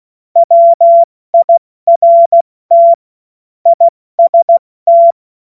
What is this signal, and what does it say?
Morse code, heard as a keyed tone, about 16 wpm, 680 Hz. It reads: WIRT IST
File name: Morsekurs / Witze / Witz 11. Mai 2021, 70 F80